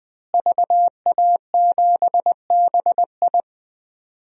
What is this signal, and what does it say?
Morse code, heard as a keyed tone, about 20 wpm, 700 Hz. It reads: VA7BI
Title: DXCC-100 Call Signs - 5 Characters - Repeated 2x - 20wpm